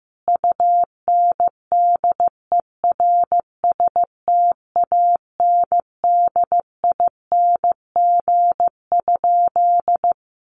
Morse code, heard as a keyed tone, about 15 wpm, 700 Hz. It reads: UNDERSTANDING?
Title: Sentences from Top 1500 Words 15wpm